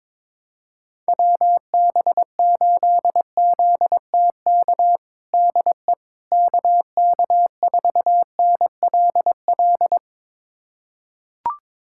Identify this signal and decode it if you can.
Morse code, heard as a keyed tone, about 22 wpm, 700 Hz. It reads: WB8ZTK DE KK4NLL
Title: USA Callsign DE Callsign - 6 Characters - 22wpm